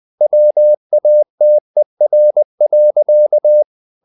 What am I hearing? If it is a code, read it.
Morse code, heard as a keyed tone, about 20 wpm, 600 Hz. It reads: WATER.
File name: the old man and the sea / the old man and the sea part 1 20wpm